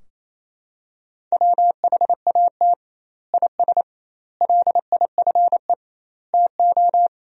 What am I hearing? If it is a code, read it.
Morse code, heard as a keyed tone, about 28 wpm, 700 Hz. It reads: WHAT IS LIFE TO
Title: 2x Word-Spacing - Sentences from Top 300 Words - 28wpm